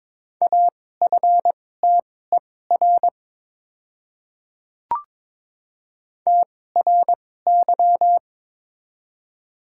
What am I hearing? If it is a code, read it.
Morse code, heard as a keyed tone, about 22 wpm, spaced as for 16 wpm, 700 Hz. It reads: AFTER TRY
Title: 2x Character Spacing - Top 200 Words - 22wpm